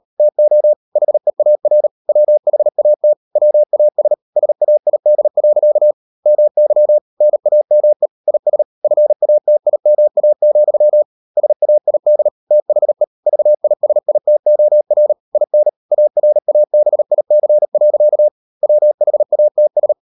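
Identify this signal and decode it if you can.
Morse code, heard as a keyed tone, about 38 wpm, 600 Hz. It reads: TO HEAR WHAT WAS SAID. MY NAME IS FATIMA, SAID THE VISITOR IN ARABIC. WHATS
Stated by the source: Morse code practice at 38 WPM, sending Ken Follett's novel Never in English, from CW med Gnister.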